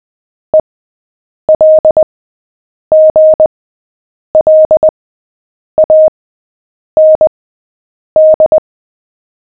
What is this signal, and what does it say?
Morse code, heard as a keyed tone, about 20 wpm, spaced as for 8 wpm, 620 Hz. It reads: ELGLAND
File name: cw / fists / Short-Story-7-8wpm